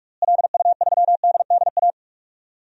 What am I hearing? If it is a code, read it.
Morse code, heard as a keyed tone, about 46 wpm, 700 Hz. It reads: LU3DDA
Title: DXCC-100 Call Signs - 6 Characters - Repeated 2x - 45wpm